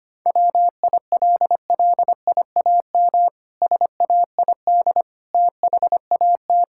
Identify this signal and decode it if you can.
Morse code, heard as a keyed tone, about 25 wpm, 700 Hz. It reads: WILLIAM SAID THAT
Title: Pangram Challenge v2 - 25wpm